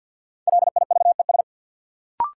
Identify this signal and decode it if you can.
Morse code, heard as a keyed tone, about 49 wpm, 700 Hz. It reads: LIVES
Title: Sets of 5 Words Encore - New Word Drill - 50wpm